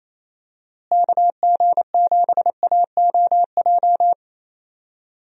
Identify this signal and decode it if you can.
Morse code, heard as a keyed tone, about 28 wpm, 700 Hz. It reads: KG7AOJ
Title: USA Call Signs - 6 Characters - Repeated 3x - 28wpm